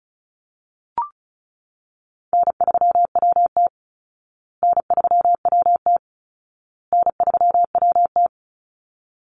Morse code, heard as a keyed tone, about 34 wpm, 700 Hz. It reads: N3WT N3WT N3WT
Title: Easy Call Signs Repeated 3x - 35wpm